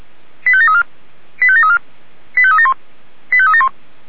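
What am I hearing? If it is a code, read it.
Five-tone selective call (ZVEI), heard as quick runs of five tones, tones of about 70 ms. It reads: 85734 85734 85371 85371
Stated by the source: ZVEI five-tone selective call received by SDR, address 89200, from a server of Feuerwehr Orlamünde (fire brigade).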